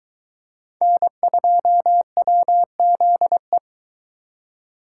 Morse code, heard as a keyed tone, about 22 wpm, 700 Hz. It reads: N2WZE